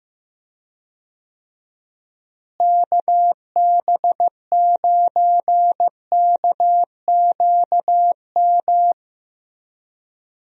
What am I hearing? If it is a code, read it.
Morse code, heard as a keyed tone, about 16 wpm, 700 Hz. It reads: KB9KQM
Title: USA Call Signs - 6 Characters - Repeated 2x - 15wpm